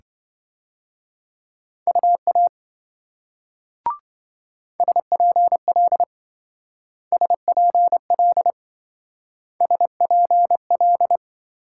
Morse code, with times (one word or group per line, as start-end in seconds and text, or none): 1.9-2.5 UA
4.8-6.0 SPL
7.1-8.5 SPL
9.6-11.2 SPL